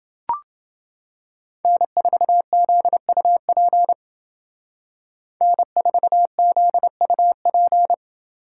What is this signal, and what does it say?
Morse code, heard as a keyed tone, about 29 wpm, 700 Hz. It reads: N4ZUP N4ZUP